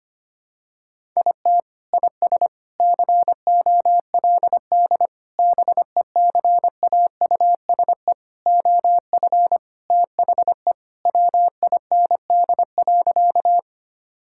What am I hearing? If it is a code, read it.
Morse code, heard as a keyed tone, about 25 wpm, 700 Hz. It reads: IT IS COLD BECAUSE OF THE WIND.
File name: practice / 25wpm / Sentences with 3-Word Phrases - 25wpm